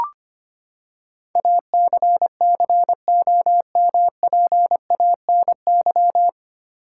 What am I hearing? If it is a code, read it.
Morse code, heard as a keyed tone, about 25 wpm, 700 Hz. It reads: ACCOMPANY
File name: Nine Letter Words - 25wpm